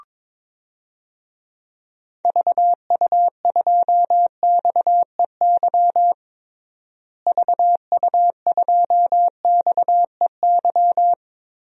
Morse code, heard as a keyed tone, about 22 wpm, 700 Hz. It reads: VU2XEY VU2XEY
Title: DXCC-100 Call Signs - 6 Characters - Repeated 2x - 22wpm